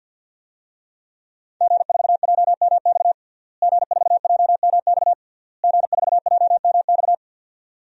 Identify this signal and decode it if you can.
Morse code, heard as a keyed tone, about 52 wpm, 700 Hz. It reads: G4JMX G4JMX G4JMX
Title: DXCC-100 Call Signs - 5 Characters - Repeated 3x - 50wpm